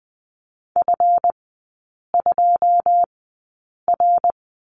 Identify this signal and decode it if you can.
Morse code, heard as a keyed tone, about 20 wpm, spaced as for 8 wpm, 700 Hz. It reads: F2R
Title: 138 Lesson 23e - Letter f - Three Characters 20wpm